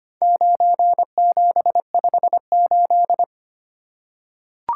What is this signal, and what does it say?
Morse code, heard as a keyed tone, about 25 wpm, 700 Hz. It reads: 9758
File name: practice / 25wpm / Four Letter-Number Combinations - 25wpm